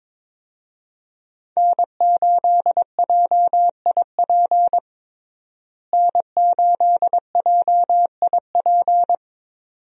Morse code, heard as a keyed tone, about 22 wpm, 700 Hz. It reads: N8JIP N8JIP
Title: USA Call Signs - 5 Characters - Repeated 2x - 22wpm